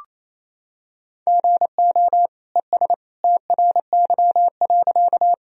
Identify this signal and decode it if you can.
Morse code, heard as a keyed tone, about 28 wpm, 700 Hz. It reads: GO ES TRY.
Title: Sentences from Top 200 Words and Abbreviations 28wpm